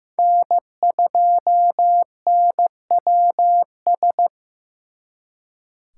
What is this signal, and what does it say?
Morse code, heard as a keyed tone, about 18 wpm, 700 Hz. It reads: N2NWS
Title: Speed Racing - USA 1x3 Call Signs 15wpm